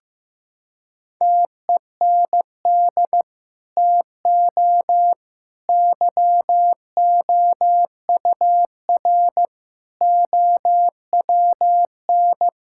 Morse code, heard as a keyed tone, about 15 wpm, 700 Hz. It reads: TEND TO YOUR OWN
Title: Sentences from Top 1000 Words - Review Part 5 - 15wpm